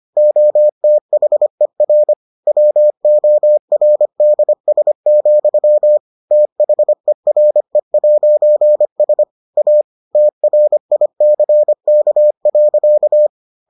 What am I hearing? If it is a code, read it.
Morse code, heard as a keyed tone, about 25 wpm, 600 Hz. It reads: OTHER WORDS, THERE'S A TRICK.